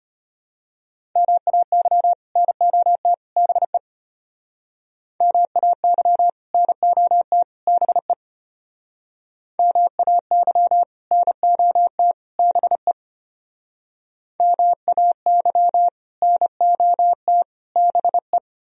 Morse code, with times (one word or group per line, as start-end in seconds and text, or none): 1.2-2.1 MAY
2.4-3.1 NOT
3.4-3.8 BE
5.2-6.3 MAY
6.5-7.4 NOT
7.7-8.1 BE
9.6-10.8 MAY
11.1-12.1 NOT
12.4-12.9 BE
14.4-15.9 MAY
16.2-17.4 NOT
17.8-18.4 BE